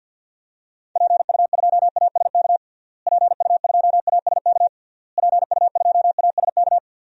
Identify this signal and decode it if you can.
Morse code, heard as a keyed tone, about 51 wpm, 700 Hz. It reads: PU2ASK PU2ASK PU2ASK
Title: DXCC-100 Call Signs - 6 Characters - Repeated 3x - 50wpm